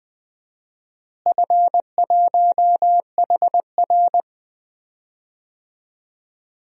Morse code, heard as a keyed tone, about 20 wpm, 700 Hz. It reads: F1HR
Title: DXCC-100 Call Signs - 4 Characters - Repeated 3x - 20wpm